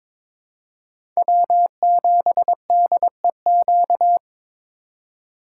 Morse code, heard as a keyed tone, about 22 wpm, 700 Hz. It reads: W7DEQ
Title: USA Call Signs - 5 Characters - 4x Character Spacing - 22wpm